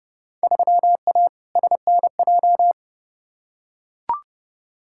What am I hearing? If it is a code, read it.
Morse code, heard as a keyed tone, about 30 wpm, 700 Hz. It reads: 3A SNJ
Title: Field Day Exchanges - 30wpm